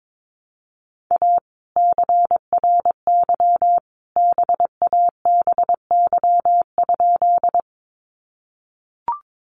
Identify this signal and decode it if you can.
Morse code, heard as a keyed tone, about 22 wpm, 700 Hz. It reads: A CRY BABY?
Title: Sentences from Top 1300 Words 22wpm